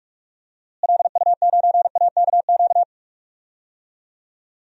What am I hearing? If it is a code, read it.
Morse code, heard as a keyed tone, about 44 wpm, 700 Hz. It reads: LU9AKQ
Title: DXCC-100 Call Signs - 6 Characters - Repeated 3x - 45wpm